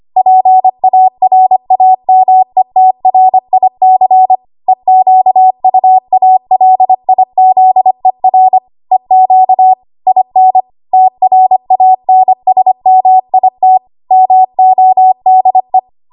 Morse code, heard as a keyed tone, about 25 wpm, 750 Hz. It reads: PARAMETRIC EQUALIZER EQ IN TRANSMIT MODE
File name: Archive / 25 WPM / 25WPM